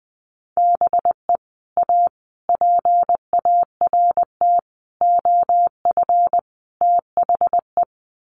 Morse code, heard as a keyed tone, about 20 wpm, 700 Hz. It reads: BE A PART OF THE
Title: Sets of 5 Words Encore 20wpm